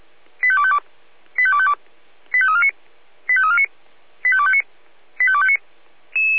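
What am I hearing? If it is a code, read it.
Five-tone selective call (ZVEI), heard as quick runs of five tones, tones of about 70 ms. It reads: 85252 85252 85349 85349 85259 85259
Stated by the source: ZVEI five-tone selective call received by SDR.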